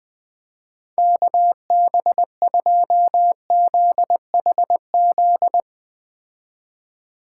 Morse code, heard as a keyed tone, about 20 wpm, 700 Hz. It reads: KB2ZHZ